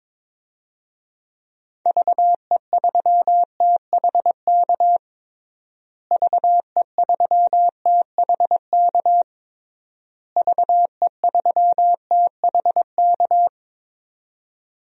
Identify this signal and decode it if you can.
Morse code, heard as a keyed tone, about 22 wpm, 700 Hz. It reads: VE3THK VE3THK VE3THK
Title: DXCC-100 Call Signs - 6 Characters - Repeated 3x - 22wpm